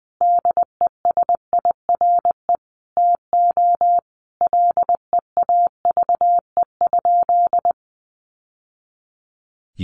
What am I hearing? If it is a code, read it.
Morse code, heard as a keyed tone, about 20 wpm, 700 Hz. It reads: DESIRE TO LEAVE?
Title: Sentences from Top 1100 Words 20wpm